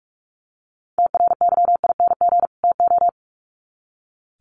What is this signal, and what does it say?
Morse code, heard as a keyed tone, about 44 wpm, 700 Hz. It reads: TRYING TO
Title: Sets of 2 Words 45wpm